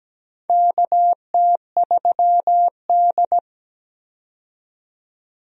Morse code, heard as a keyed tone, about 17 wpm, 700 Hz. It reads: KT3D
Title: USA Call Signs - 4 Characters - 17wpm